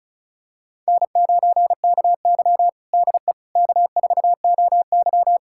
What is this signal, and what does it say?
Morse code, heard as a keyed tone, about 36 wpm, 700 Hz. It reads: N9KY DE K4OY
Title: USA Callsign DE Callsign - 4 Characters - 35wpm